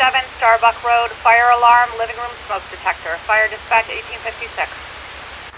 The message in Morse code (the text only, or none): none